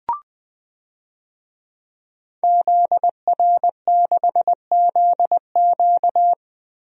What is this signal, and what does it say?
Morse code, heard as a keyed tone, about 20 wpm, 700 Hz. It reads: ZR6ZQ
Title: DXCC-100 Call Signs - 5 Characters - 20wpm